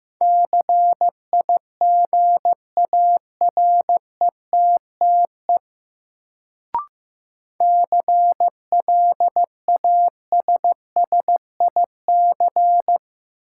Morse code, heard as a keyed tone, about 15 wpm, 700 Hz. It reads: CIGARETTE CLASSIC
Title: Sentences from Top 1000 Words - New Word Drill - 15wpm